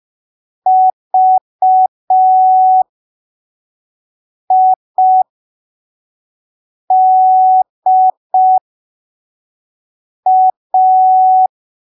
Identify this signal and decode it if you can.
Morse code, heard as a keyed tone, about 5 wpm, spaced as for 3 wpm, 750 Hz. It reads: VIDA